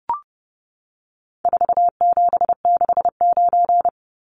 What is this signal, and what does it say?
Morse code, heard as a keyed tone, about 30 wpm, 700 Hz. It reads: 4769
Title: Numbers - 4-Digit 30wpm